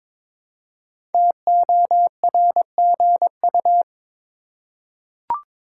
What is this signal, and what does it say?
Morse code, heard as a keyed tone, about 22 wpm, 700 Hz. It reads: TORGU
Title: Code Groups - 5 Letters - 22wpm